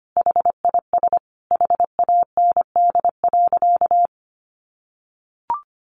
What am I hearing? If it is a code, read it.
Morse code, heard as a keyed tone, about 25 wpm, 700 Hz. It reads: HIS HAND.
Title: Sentences from Top 1400 Words 25wpm